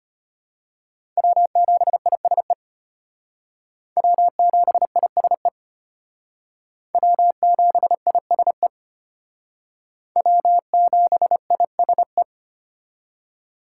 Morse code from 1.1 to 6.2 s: W7ISE W7ISE

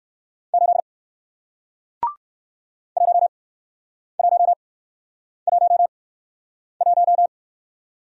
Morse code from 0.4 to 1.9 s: 8

